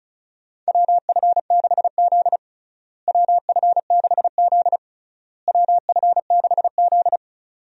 Morse code, heard as a keyed tone, about 34 wpm, 700 Hz. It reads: WF6Z WF6Z WF6Z
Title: USA Call Signs - 4 Characters - Repeated 3x - 35wpm